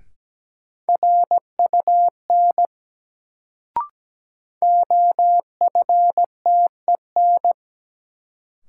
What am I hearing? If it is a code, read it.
Morse code, heard as a keyed tone, about 17 wpm, 700 Hz. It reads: RUN OFTEN
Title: Sentences from Top 300 Words - New Word Drill - 17wpm